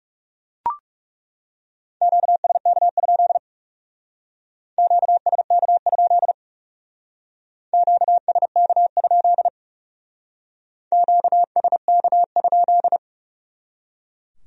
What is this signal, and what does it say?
Morse code, heard as a keyed tone, about 37 wpm, 700 Hz. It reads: QSK? QSK? QSK? QSK?
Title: Speed-Racing Q-Codes - 30wpm